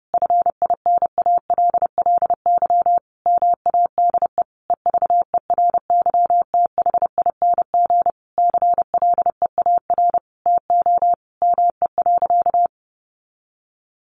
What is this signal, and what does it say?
Morse code, heard as a keyed tone, about 30 wpm, 700 Hz. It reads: FINALLY MADE EVERYTHING CLEAR TO ME.
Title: Sentences from Top 1300 Words 30wpm